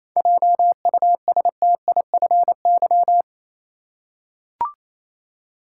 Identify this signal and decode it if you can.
Morse code, heard as a keyed tone, about 28 wpm, 700 Hz. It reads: JUSTIFY
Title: Seven Letter Words - 28wpm